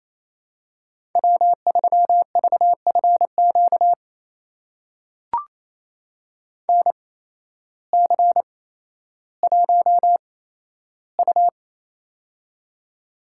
Morse code, heard as a keyed tone, about 28 wpm, spaced as for 13 wpm, 700 Hz. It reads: W3VFQ NC1U
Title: USA Call Signs - 8x Character Spacing - 28wpm